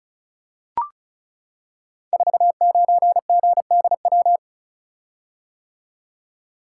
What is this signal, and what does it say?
Morse code, heard as a keyed tone, about 36 wpm, 690 Hz. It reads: 49GDW
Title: Code Groups - 5 Letter:Numbers - 35wpm